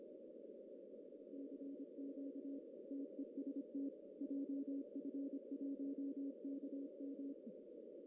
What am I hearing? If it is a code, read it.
Morse code, heard as a keyed tone, about 25 wpm, 280 Hz. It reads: CQ TEST JF1KM
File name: CW / cqww-cw-k5tr